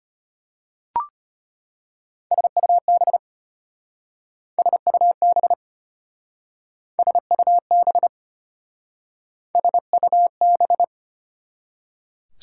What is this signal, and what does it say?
Morse code, heard as a keyed tone, about 30 wpm, 690 Hz. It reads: SUB SUB SUB SUB
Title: Speed-Racing Prefixes and Suffixes 25wpm